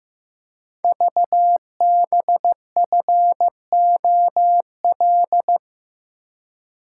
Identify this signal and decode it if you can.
Morse code, heard as a keyed tone, about 15 wpm, 700 Hz. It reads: VBFOL
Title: Code Groups - 5 Letters - 15wpm